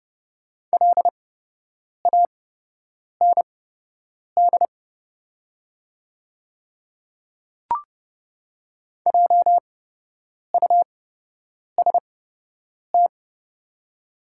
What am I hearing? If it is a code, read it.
Morse code, heard as a keyed tone, about 30 wpm, spaced as for 8 wpm, 700 Hz. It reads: LAND JUST
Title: 8x Character Spacing - Top 300 Words - 30wpm